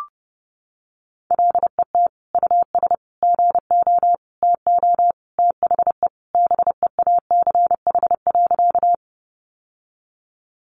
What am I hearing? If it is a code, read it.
Morse code, heard as a keyed tone, about 28 wpm, 700 Hz. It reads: LET US GO TO THE BEACH.
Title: Sentences from Top 1500 Words 30wpm